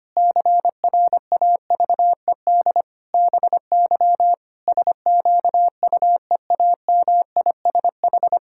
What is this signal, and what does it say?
Morse code, heard as a keyed tone, about 25 wpm, 700 Hz. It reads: CRAVED BY SQUEAMISH
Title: Pangram Challenge v2 - 25wpm